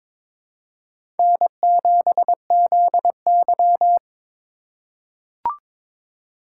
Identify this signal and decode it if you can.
Morse code, heard as a keyed tone, about 22 wpm, 700 Hz. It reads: N7ZY
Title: USA Call Signs - 4 Characters - 3x Character Spacing - 22wpm